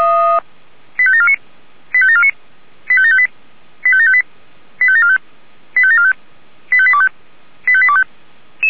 Five-tone selective call (ZVEI), heard as quick runs of five tones, tones of about 80 ms. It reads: 85749 85749 85758 85758 85745 85745 85825 85825